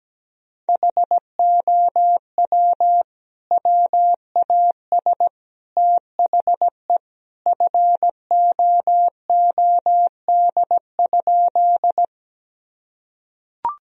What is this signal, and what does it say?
Morse code, heard as a keyed tone, about 17 wpm, 700 Hz. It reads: HOW WAS THE FOOD?
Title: Sentences from Top 1000 Words - Review Part 1 - 17wpm